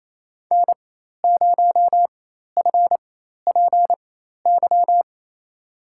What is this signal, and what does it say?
Morse code, heard as a keyed tone, about 28 wpm, spaced as for 13 wpm, 700 Hz. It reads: N0FPY
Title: USA Call Signs - 5 Characters - 4x Character Spacing - 28wpm